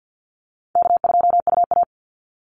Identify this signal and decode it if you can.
Morse code, heard as a keyed tone, about 49 wpm, 700 Hz. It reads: K2UA
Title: Call Signs Repeated 2x - 50wpm